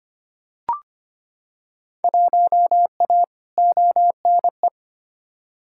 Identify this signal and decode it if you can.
Morse code, heard as a keyed tone, about 25 wpm, 700 Hz. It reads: 1A ONE
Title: Field Day Exchanges Encore - 25wpm